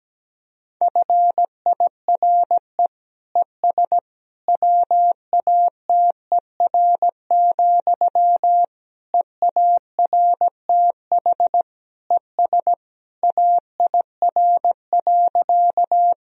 Morse code, heard as a keyed tone, about 17 wpm, 700 Hz. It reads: FIRE ES WATER, EARTH ES AIR.
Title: Sentences from Top 400 Words and Abbreviations 17wpm